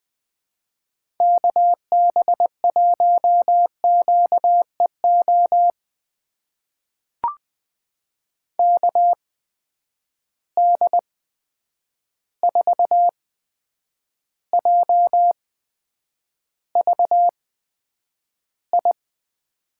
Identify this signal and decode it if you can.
Morse code, heard as a keyed tone, about 21 wpm, spaced as for 9 wpm, 700 Hz. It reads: KB1QEO KD4JVI
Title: USA Call Signs - 6 Characters - 8x Character Spacing - 20wpm